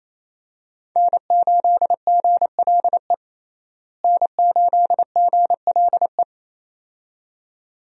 Morse code, heard as a keyed tone, about 28 wpm, 700 Hz. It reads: N8GLE N8GLE
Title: USA Call Signs - 5 Characters - Repeated 2x - 28wpm